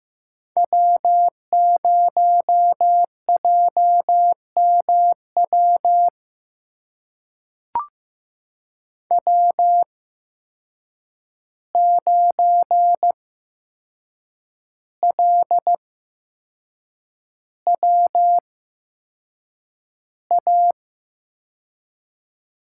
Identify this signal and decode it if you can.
Morse code, heard as a keyed tone, about 15 wpm, spaced as for 6 wpm, 700 Hz. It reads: W0JMW W9LWA
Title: USA Call Signs - 8x Character Spacing - 15wpm